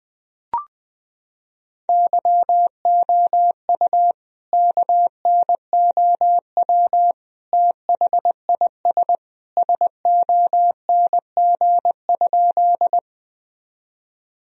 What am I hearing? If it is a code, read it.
Morse code, heard as a keyed tone, about 20 wpm, 700 Hz. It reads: YOU KNOW THIS SONG?
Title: Sentences from Top 1000 Words - Review Part 2 - 20wpm